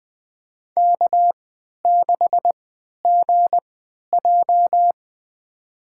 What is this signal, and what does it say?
Morse code, heard as a keyed tone, about 20 wpm, spaced as for 11 wpm, 700 Hz. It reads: K6GJ
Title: USA Call Signs - 4 Characters - 3x Character Spacing - 20wpm